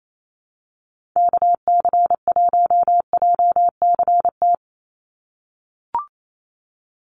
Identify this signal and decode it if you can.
Morse code, heard as a keyed tone, about 28 wpm, 700 Hz. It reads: KC1JCT
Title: USA Call Signs 28wpm